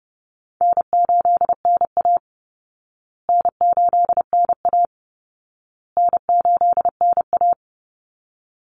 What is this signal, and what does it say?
Morse code, heard as a keyed tone, about 30 wpm, 700 Hz. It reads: N8NA N8NA N8NA
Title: Easy Call Signs Repeated 3x - 30wpm